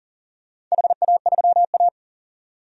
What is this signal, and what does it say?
Morse code, heard as a keyed tone, about 38 wpm, 700 Hz. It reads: HA3A
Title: DXCC-100 Call Signs - 4 Characters - 40wpm